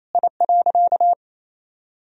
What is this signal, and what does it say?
Morse code, heard as a keyed tone, about 28 wpm, 700 Hz. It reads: I.
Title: Sentences from Top 1000 Words - Review Part 2 - 28wpm